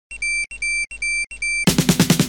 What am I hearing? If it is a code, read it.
There is no code.